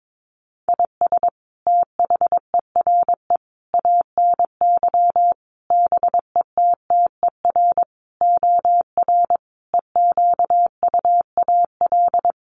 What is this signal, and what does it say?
Morse code, heard as a keyed tone, about 22 wpm, 700 Hz. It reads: IS THERE ANY BETTER OR EQUAL
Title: Sentences from Top 1400 Words 22wpm